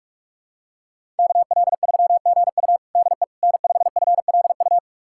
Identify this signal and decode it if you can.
Morse code, heard as a keyed tone, about 41 wpm, 700 Hz. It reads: KL3GU DE N5FLU
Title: USA Callsign DE Callsign - 5 Characters - 45wpm